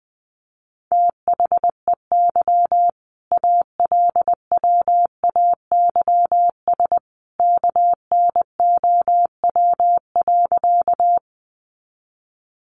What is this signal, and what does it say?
Morse code, heard as a keyed tone, about 20 wpm, 700 Hz. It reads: THEY ALWAYS KNOW.
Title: Sentences from Top 300 Words 20wpm